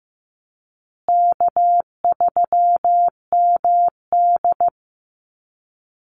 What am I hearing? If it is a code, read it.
Morse code, heard as a keyed tone, about 15 wpm, 700 Hz. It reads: K3MD
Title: Easy Call Signs Repeated 3x - 15wpm